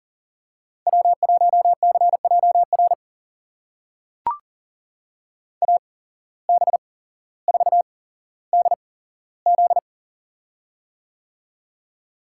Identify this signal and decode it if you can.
Morse code, heard as a keyed tone, about 40 wpm, spaced as for 17 wpm, 700 Hz. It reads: W1CJR AB4DZ
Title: USA Call Signs - 5 Characters - 8x Character Spacing - 40wpm